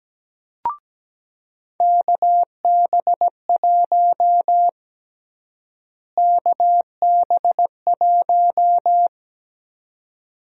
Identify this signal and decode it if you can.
Morse code, heard as a keyed tone, about 17 wpm, 700 Hz. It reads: KB1 KB1
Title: USA Call Sign Prefixes - Repeated 2x - 17wpm